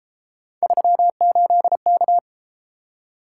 Morse code, heard as a keyed tone, about 31 wpm, 700 Hz. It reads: 38K